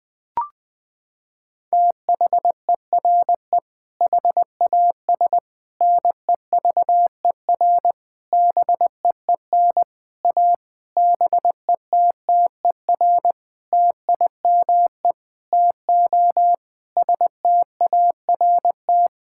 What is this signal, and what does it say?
Morse code, heard as a keyed tone, about 20 wpm, 700 Hz. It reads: THERE HAS NEVER BEEN A BETTER TIME TO START
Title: Sentences with 7-Word Phrases - 20wpm